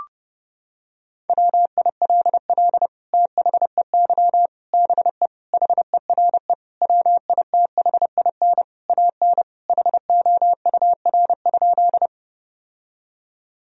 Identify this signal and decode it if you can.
Morse code, heard as a keyed tone, about 32 wpm, 700 Hz. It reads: WILL THEY BE HERE WITHIN AN HOUR?
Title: Sentences from Top 1000 Words - Review Part 4 - 30wpm